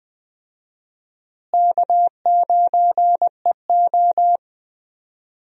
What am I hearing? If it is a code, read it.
Morse code, heard as a keyed tone, about 20 wpm, 700 Hz. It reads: K9EO
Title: USA Call Signs - 4 Characters - Repeated 3x - 20wpm